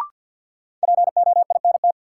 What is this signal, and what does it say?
Morse code, heard as a keyed tone, about 53 wpm, 700 Hz. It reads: POINT
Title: Sentences from Top 200 Words - New Word Drill - 50wpm